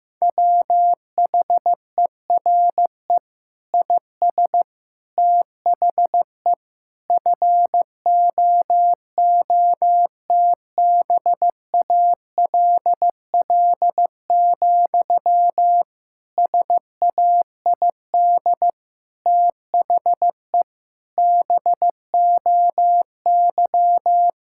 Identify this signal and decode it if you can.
Morse code, heard as a keyed tone, about 15 wpm, 700 Hz. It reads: WHERE IS THE FOOTBALL, SAID THE BOY
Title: Sentences from Top 1600 Words 15wpm